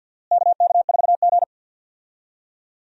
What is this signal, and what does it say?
Morse code, heard as a keyed tone, about 52 wpm, 700 Hz. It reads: KK4G
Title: USA Call Signs - 4 Characters - Repeated 3x - 50wpm